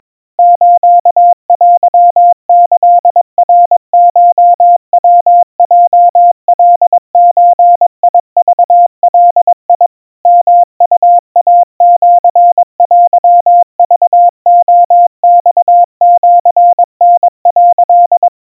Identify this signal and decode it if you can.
Morse code (Japanese Wabun code), heard as a keyed tone, about 22 wpm, 700 Hz. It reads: ステキナコヤヲカゾクガ ヨウイシテクレマシタ」
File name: 80 cpm 700 Hz Crispy